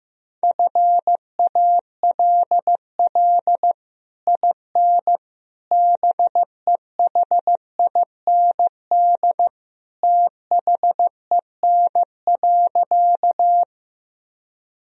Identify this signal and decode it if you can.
Morse code, heard as a keyed tone, about 15 wpm, 700 Hz. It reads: FALL IN BEHIND THEN.
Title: Sentences from Top 1000 Words - Review Part 2 - 15wpm